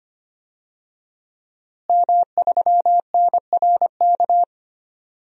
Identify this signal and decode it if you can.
Morse code, heard as a keyed tone, about 25 wpm, 700 Hz. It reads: M3NRK
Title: DXCC-100 Call Signs - 5 Characters - Repeated 3x - 25wpm